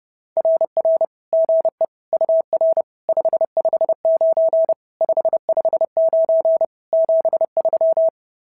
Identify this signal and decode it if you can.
Morse code, heard as a keyed tone, about 30 wpm, 650 Hz. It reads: RR GE UR 559 559 73
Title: SOTA QSO Practice 30wpm